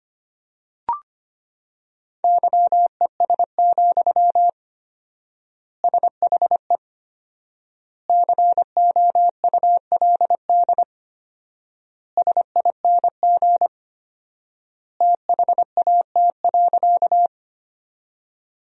Morse code, heard as a keyed tone, about 25 wpm, 700 Hz. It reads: YES, SHE COULD SING THAT.
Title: 4x Word-Spacing - Sentences from Top 500 Words - 25wpm